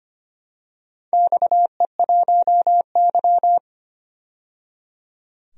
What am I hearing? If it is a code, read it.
Morse code, heard as a keyed tone, about 25 wpm, 700 Hz. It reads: XE1Y
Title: DXCC-100 Call Signs - 4 Characters - 25wpm